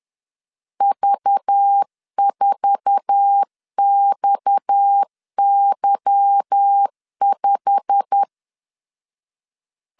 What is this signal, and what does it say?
Morse code, heard as a keyed tone, about 10 wpm, 800 Hz. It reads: V4XY5
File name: LEZIONI / 8 YJ45 / 8 YJ45 10wpm 2549